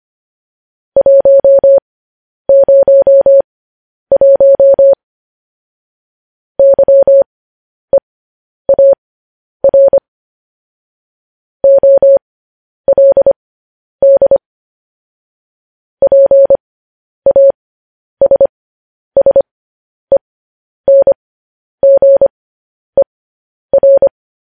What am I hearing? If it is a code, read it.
Morse code, heard as a keyed tone, about 25 wpm, spaced as for 10 wpm, 550 Hz. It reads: 101 YEAR OLD PASSENGER
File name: podcast / News headlines-1x-25x10